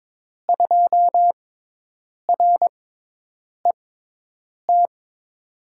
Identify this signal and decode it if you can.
Morse code, heard as a keyed tone, about 22 wpm, spaced as for 8 wpm, 700 Hz. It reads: 2RET
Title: USA Call Signs - 6x Character Spacing - 22wpm